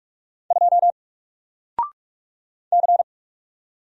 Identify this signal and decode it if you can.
Morse code, heard as a keyed tone, about 46 wpm, 700 Hz. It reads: 2 C
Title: Single Character (Letter:Number:Punctuation:Prosign) 45wpm